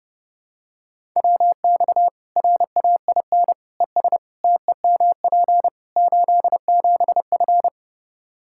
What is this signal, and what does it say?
Morse code, heard as a keyed tone, about 30 wpm, 700 Hz. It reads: WX RAIN ES TEMP 87F
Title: Practice QSO - Exchange 2 - Info - 30wpm